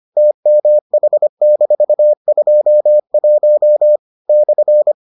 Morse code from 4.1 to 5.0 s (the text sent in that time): /